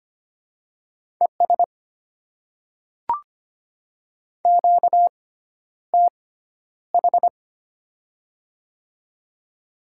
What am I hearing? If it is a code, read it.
Morse code, heard as a keyed tone, about 25 wpm, spaced as for 11 wpm, 700 Hz. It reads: ES QTH
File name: Top 20 QSO Elements - 6x Character Spacing - 25wpm